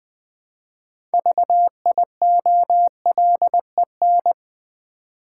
Morse code, heard as a keyed tone, about 20 wpm, 700 Hz. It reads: VIOLEN
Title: Build A Word - 3 Letters At A Time - 8-Letter Words - 20wpm